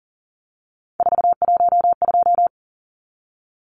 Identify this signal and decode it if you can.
Morse code, heard as a keyed tone, about 40 wpm, 700 Hz. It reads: 412
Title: Numbers - 3-Digit 40wpm